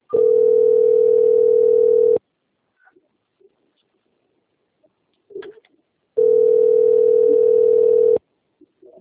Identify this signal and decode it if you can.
Ringback tone (North America), 440 + 480 Hz.
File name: out- - - - - .18848